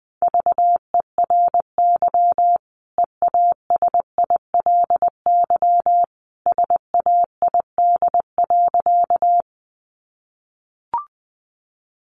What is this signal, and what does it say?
Morse code, heard as a keyed tone, about 20 wpm, 700 Hz. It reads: VERY EASILY SAID.
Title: Sentences from Top 900 Words 20wpm